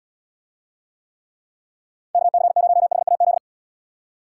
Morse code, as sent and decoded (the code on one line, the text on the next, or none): -.- -.-. ----- .... .- --..
KC0HAZ